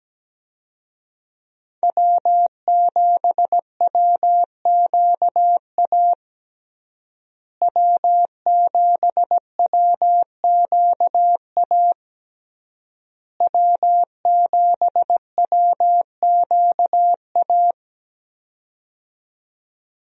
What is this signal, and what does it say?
Morse code, heard as a keyed tone, about 17 wpm, 700 Hz. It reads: W7WQA W7WQA W7WQA
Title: USA Call Signs - 5 Characters - Repeated 3x - 17wpm